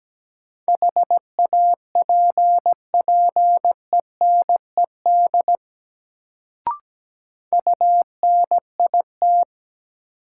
Morse code, heard as a keyed tone, about 17 wpm, 700 Hz. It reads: HAPPENED UNIT
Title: Sentences from Top 1000 Words - Review Part 2 - New Word Drill - 17wpm